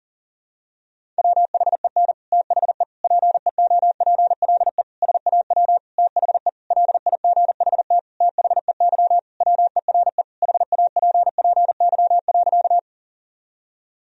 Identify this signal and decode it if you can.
Morse code, heard as a keyed tone, about 40 wpm, 700 Hz. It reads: WHEN THE PEOPLE SAW THE LIGHT THEY WERE HAPPY.
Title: Sentences from Top 1000 Words - Review Part 1 - 40wpm